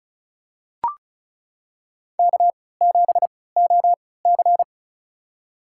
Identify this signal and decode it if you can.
Morse code, heard as a keyed tone, about 36 wpm, spaced as for 20 wpm, 700 Hz. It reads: K7OC
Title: USA Call Signs - 4 Characters - 3x Character Spacing - 35wpm